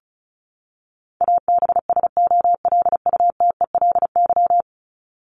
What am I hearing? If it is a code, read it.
Morse code, heard as a keyed tone, about 34 wpm, 700 Hz. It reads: ABSOLUTELY
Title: LY Suffix - Top 100 Words with Base Words 35wpm